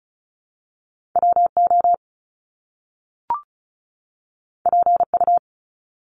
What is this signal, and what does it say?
Morse code, heard as a keyed tone, about 36 wpm, 700 Hz. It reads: WO PU
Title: Top Two Letter Combinations in Words 35wpm